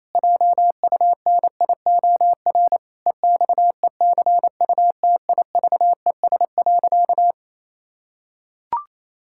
Morse code, heard as a keyed tone, about 27 wpm, 700 Hz. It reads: JUNIOR EXECUTIVES.
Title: Pangram Challenge v2 - 28wpm